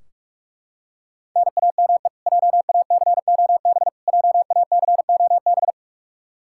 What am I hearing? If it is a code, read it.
Morse code, heard as a keyed tone, about 42 wpm, 700 Hz. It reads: NAME JACOB JACOB
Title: K1USN SST QSO Practice 45wpm